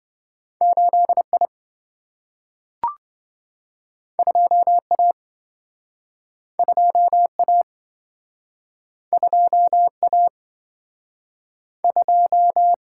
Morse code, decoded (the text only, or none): 8I 2A 2A 2A 2